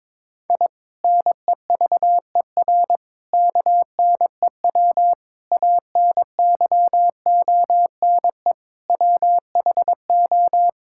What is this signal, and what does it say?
Morse code, heard as a keyed tone, about 22 wpm, 700 Hz. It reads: I NEVER KNEW ANYONE WHO
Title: Sentences from Top 1000 Words - Review Part 6 - 22wpm